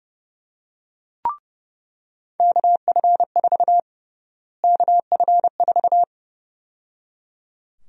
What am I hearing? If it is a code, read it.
Morse code, heard as a keyed tone, about 32 wpm, 700 Hz. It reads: KF4 KF4
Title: USA Call Sign Prefixes - Repeated 2x - 30wpm